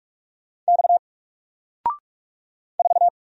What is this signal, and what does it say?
Morse code, heard as a keyed tone, about 46 wpm, 700 Hz. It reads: X 4
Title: Single Letter-Number 45wpm